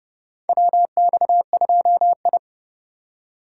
Morse code, heard as a keyed tone, about 30 wpm, 700 Hz. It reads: WX2I
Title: USA Call Signs - 3x Character Spacing - 30wpm